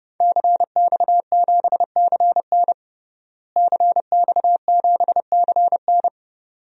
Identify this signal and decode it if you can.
Morse code, heard as a keyed tone, about 30 wpm, 700 Hz. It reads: CX7CN CX7CN
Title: DXCC-100 Call Signs - 5 Characters - Repeated 2x - 30wpm